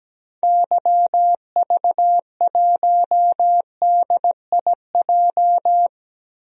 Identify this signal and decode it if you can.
Morse code, heard as a keyed tone, about 17 wpm, 700 Hz. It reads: YV1DIJ
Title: DXCC-100 Call Signs - 6 Characters - 17wpm